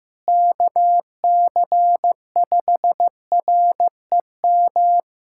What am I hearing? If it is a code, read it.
Morse code, heard as a keyed tone, about 15 wpm, 700 Hz. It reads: KC5REM
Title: USA Callsign DE Callsign 15wpm